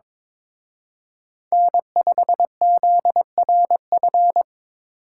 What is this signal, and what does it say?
Morse code, heard as a keyed tone, about 26 wpm, 700 Hz. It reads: N5ZRF
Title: Speed Racing - USA 1x3 Call Signs 22wpm